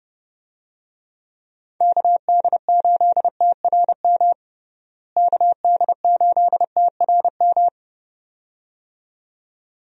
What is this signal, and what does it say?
Morse code, heard as a keyed tone, about 32 wpm, 700 Hz. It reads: KD8TRM KD8TRM
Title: USA Call Signs - 6 Characters - Repeated 2x - 30wpm